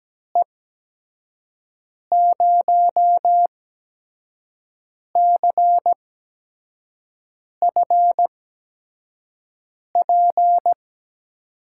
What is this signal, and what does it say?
Morse code, heard as a keyed tone, about 17 wpm, spaced as for 5 wpm, 700 Hz. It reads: E0CFP
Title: USA Call Signs - 6 Characters - 8x Character Spacing - 17wpm